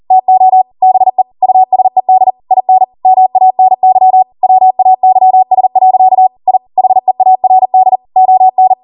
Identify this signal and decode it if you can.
Morse code, heard as a keyed tone, about 40 wpm, 750 Hz. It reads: TO BE USED IN MANY WAYS. I HEARD ON